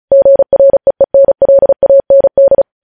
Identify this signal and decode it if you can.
Morse code, heard as a keyed tone, about 36 wpm, 550 Hz. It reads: GREENLAND
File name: podcast / News headlines-1x-35x35